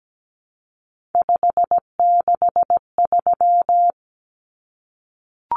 Morse code, heard as a keyed tone, about 17 wpm, 700 Hz. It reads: 563
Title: Common Cut Numbers - 3-Digit 17wpm